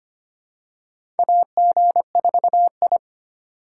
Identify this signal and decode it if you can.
Morse code, heard as a keyed tone, about 25 wpm, 700 Hz. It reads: AG4I
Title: USA Call Signs - 4 Characters - 2x Character Spacing - 25wpm